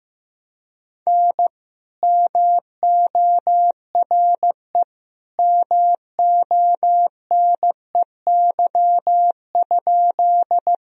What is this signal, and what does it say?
Morse code, heard as a keyed tone, about 15 wpm, 700 Hz. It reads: N MORE MONEY?
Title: Sentences from Top 400 Words and Abbreviations 15wpm